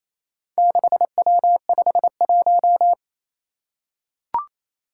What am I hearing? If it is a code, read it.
Morse code, heard as a keyed tone, about 28 wpm, 700 Hz. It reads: 6W51
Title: Four Challenging Letter-Number Combinations 28wpm